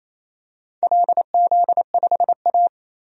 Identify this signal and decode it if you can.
Morse code, heard as a keyed tone, about 28 wpm, 690 Hz. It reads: LZ5A
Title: DXCC-100 Call Signs - 4 Characters - 28wpm